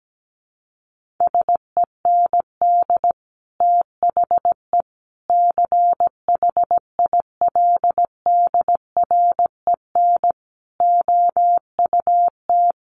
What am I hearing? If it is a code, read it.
Morse code, heard as a keyed tone, about 17 wpm, 700 Hz. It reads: SEND THE CHILDREN OUT
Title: Sentences from Top 1500 Words 17wpm